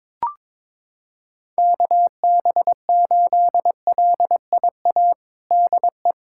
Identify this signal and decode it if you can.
Morse code, heard as a keyed tone, about 22 wpm, 700 Hz. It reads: KB8LIA DE
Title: USA Callsign DE Callsign - 6 Characters - 22wpm